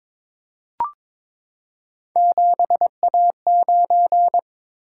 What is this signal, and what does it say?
Morse code, heard as a keyed tone, about 22 wpm, 700 Hz. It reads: 7A9
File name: practice / 22wpm / Three Letter-Number Combinations - 22wpm